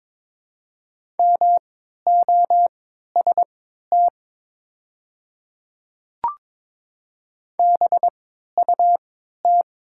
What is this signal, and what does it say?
Morse code, heard as a keyed tone, about 22 wpm, spaced as for 12 wpm, 700 Hz. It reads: MOST BUT
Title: 3x Character Spacing - Top 100 Words - 22wpm